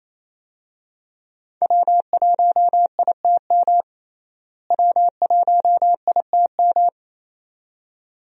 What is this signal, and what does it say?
Morse code, heard as a keyed tone, about 28 wpm, 700 Hz. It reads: W1ITM W1ITM
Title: USA Call Signs - 5 Characters - Repeated 2x - 28wpm